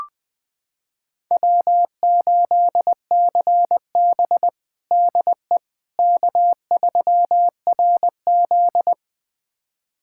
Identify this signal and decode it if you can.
Morse code, heard as a keyed tone, about 20 wpm, 700 Hz. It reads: W8CB DE K3RZ
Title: USA Callsign DE Callsign - 4 Characters - 20wpm